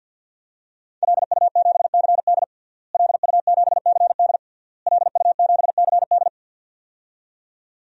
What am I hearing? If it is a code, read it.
Morse code, heard as a keyed tone, about 51 wpm, 700 Hz. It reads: LU7CD LU7CD LU7CD
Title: DXCC-100 Call Signs - 5 Characters - Repeated 3x - 50wpm